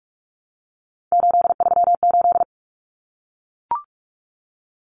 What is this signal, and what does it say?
Morse code, heard as a keyed tone, about 44 wpm, 700 Hz. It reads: 838
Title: Numbers - 3-Digit 45wpm